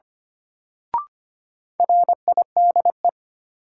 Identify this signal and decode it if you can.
Morse code, heard as a keyed tone, about 25 wpm, 700 Hz. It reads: RIDE
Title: Sentences from Top 1000 Words - Review Part 6 - New Word Drill - 25wpm